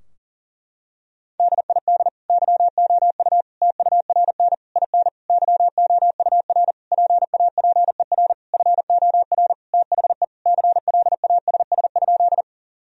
Morse code, heard as a keyed tone, about 40 wpm, 700 Hz. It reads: DID YOU TURN IN YOUR PAPER FOR THE CLASS?
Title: Sentences from Top 1000 Words - Review Part 2 - 40wpm